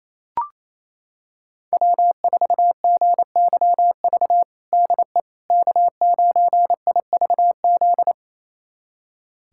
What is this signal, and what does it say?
Morse code, heard as a keyed tone, about 28 wpm, 700 Hz. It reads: W4GYV DE K9IVZ
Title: USA Callsign DE Callsign - 5 Characters - 28wpm